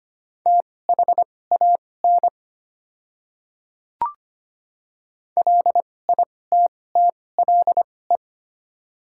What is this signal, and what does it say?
Morse code, heard as a keyed tone, about 25 wpm, spaced as for 18 wpm, 700 Hz. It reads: THAN LITTLE
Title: 2x Character Spacing - Top 200 Words - 25wpm